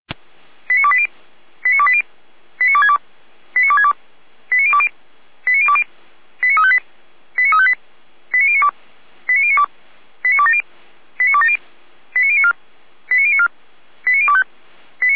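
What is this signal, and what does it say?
Five-tone selective call (ZVEI), heard as quick runs of five tones, tones of about 70 ms. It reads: 89280 89280 89262 89262 89020 89020 89358 89358 89092 89092 89270 89270 89094 89094 89025
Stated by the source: ZVEI five-tone selective call received with an SDR.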